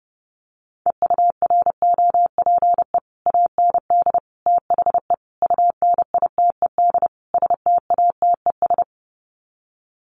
Morse code, heard as a keyed tone, about 30 wpm, 700 Hz. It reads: EUROPE AND THE UNITED STATES
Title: Sets of 5 Words Encore2 30wpm